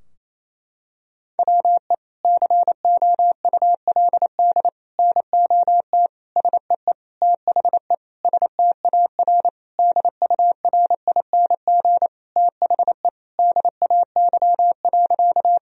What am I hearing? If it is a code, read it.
Morse code, heard as a keyed tone, about 28 wpm, 700 Hz. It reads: WE COULD NOT SEE THE STAR DURING THE DAY.
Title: Sentences from Top 1000 Words - Review Part 2 - 28wpm